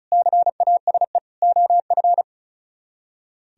Morse code, heard as a keyed tone, about 35 wpm, 700 Hz. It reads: CASE OF
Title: Speed-Racing - Sets of 4 Words 35wpm